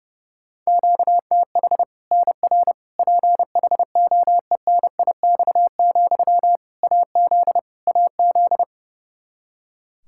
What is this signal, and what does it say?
Morse code, heard as a keyed tone, about 30 wpm, 700 Hz. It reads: QTH NR PHOENIX, AZ AZ
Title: K1USN SST QSO Practice 30wpm